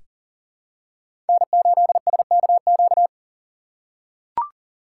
Morse code, heard as a keyed tone, about 40 wpm, 700 Hz. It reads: N8SKQ
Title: USA Call Signs - 5 Characters - 4x Character Spacing - 40wpm